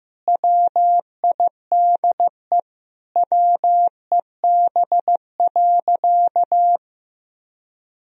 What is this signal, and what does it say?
Morse code, heard as a keyed tone, about 15 wpm, 700 Hz. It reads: WIDE WEB.
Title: Sentences from Top 1000 Words - Review Part 4 - 15wpm